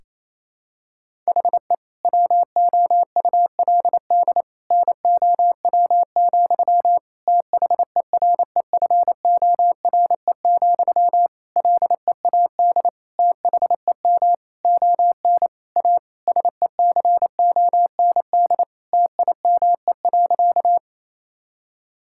Morse code, heard as a keyed tone, about 28 wpm, 700 Hz. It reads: HE WOULD NOW, THEREFORE, LEAD THEM ON A SECOND TIME.
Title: Sentences from Top 1000 Words - Review Part 5 - 28wpm